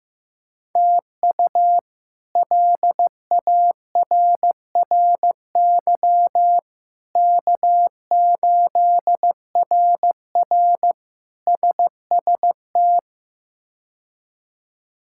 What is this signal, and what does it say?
Morse code, heard as a keyed tone, about 15 wpm, 700 Hz. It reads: TU LARRY K8RR SST
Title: K1USN SST QSO Practice - 15wpm